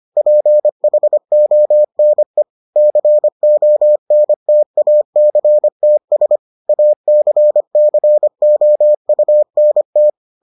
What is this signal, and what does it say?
Morse code, heard as a keyed tone, about 25 wpm, 600 Hz. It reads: PHONE CONTACTS ACCOUNT